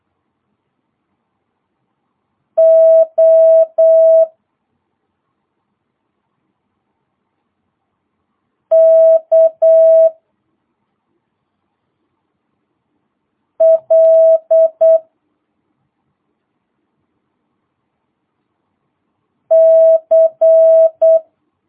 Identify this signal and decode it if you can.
Morse code, heard as a keyed tone, about 8 wpm, 650 Hz. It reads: O K L C